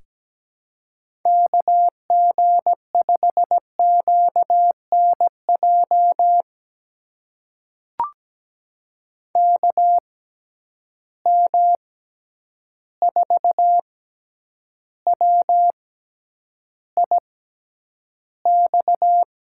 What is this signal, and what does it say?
Morse code, heard as a keyed tone, about 17 wpm, spaced as for 9 wpm, 700 Hz. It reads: KG5QNJ KM4WIX
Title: USA Call Signs - 6 Characters - 6x Character Spacing - 17wpm